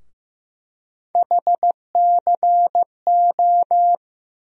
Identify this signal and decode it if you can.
Morse code, heard as a keyed tone, about 15 wpm, 700 Hz. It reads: HCO
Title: USA Call Sign Postfixes - 3 Letters 15wpm